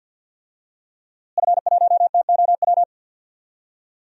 Morse code, heard as a keyed tone, about 46 wpm, 700 Hz. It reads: F1TOW
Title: DXCC-100 Call Signs - 5 Characters - Repeated 2x - 50wpm